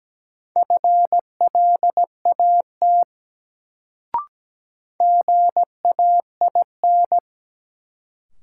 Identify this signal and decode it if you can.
Morse code, heard as a keyed tone, about 17 wpm, 700 Hz. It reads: FLAT GAIN
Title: Sentences from Top 1000 Words - New Word Drill - 17wpm